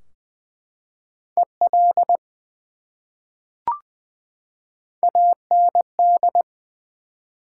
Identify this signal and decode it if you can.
Morse code, heard as a keyed tone, about 20 wpm, 700 Hz. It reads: EL AND
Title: Top 180 QSO Elements 20wpm